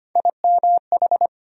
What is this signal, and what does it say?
Morse code, heard as a keyed tone, about 25 wpm, 700 Hz. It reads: IMH
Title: USA Call Sign Postfixes - 3 Letters 25wpm